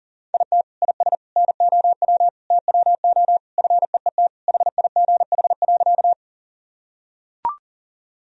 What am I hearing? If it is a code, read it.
Morse code, heard as a keyed tone, about 40 wpm, 700 Hz. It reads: IT IS NOW TWO FEET HIGH.
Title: Sentences from Top 1000 Words - Review Part 1 - 40wpm